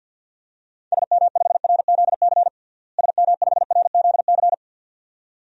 Morse code, heard as a keyed tone, about 52 wpm, 700 Hz. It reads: SM5RZC SM5RZC
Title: DXCC-100 Call Signs - 6 Characters - Repeated 2x - 50wpm